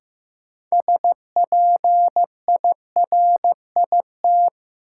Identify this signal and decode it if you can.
Morse code, heard as a keyed tone, about 16 wpm, 700 Hz. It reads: SPIRIT
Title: Sentences from Top 1400 Words - New Word Drill - 15wpm